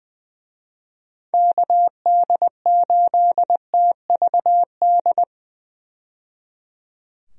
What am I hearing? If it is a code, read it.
Morse code, heard as a keyed tone, about 20 wpm, 700 Hz. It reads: KD8TVD
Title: USA Call Signs - 6 Characters - 20wpm